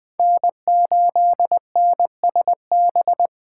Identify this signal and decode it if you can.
Morse code, heard as a keyed tone, about 21 wpm, 700 Hz. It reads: N8NSB
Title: USA Call Signs - 5 Characters - 3x Character Spacing - 20wpm